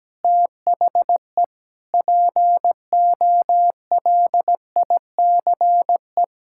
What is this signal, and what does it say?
Morse code, heard as a keyed tone, about 17 wpm, 700 Hz. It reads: THE POLICE
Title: Pangram Challenge v2 - 17wpm